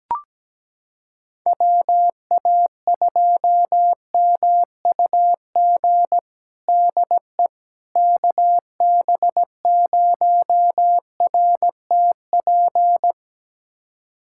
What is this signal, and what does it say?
Morse code, heard as a keyed tone, about 17 wpm, 700 Hz. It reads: WA2MUG DE KB0RTP
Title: USA Callsign DE Callsign - 6 Characters - 17wpm